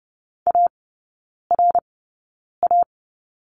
Morse code, heard as a keyed tone, about 30 wpm, spaced as for 9 wpm, 700 Hz. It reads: ARA
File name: 081 Lesson 14e - Number 2 - Three Characters 30wpm